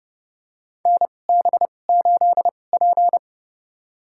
Morse code, heard as a keyed tone, about 30 wpm, spaced as for 22 wpm, 700 Hz. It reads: NB8P
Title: USA Call Signs - 4 Characters - 2x Character Spacing - 30wpm